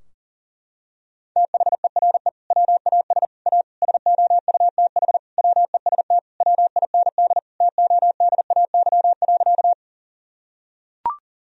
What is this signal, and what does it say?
Morse code, heard as a keyed tone, about 40 wpm, 700 Hz. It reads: THERE WAS A SOUTH WEST WIND TODAY.
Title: Sentences from Top 1000 Words - Review Part 2 - 40wpm